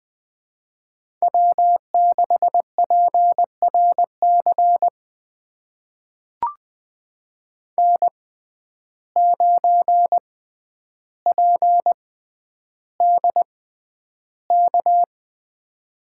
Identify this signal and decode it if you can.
Morse code, heard as a keyed tone, about 20 wpm, spaced as for 10 wpm, 700 Hz. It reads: W6PRC N9PDK
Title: USA Call Signs - 5 Characters - 6x Character Spacing - 20wpm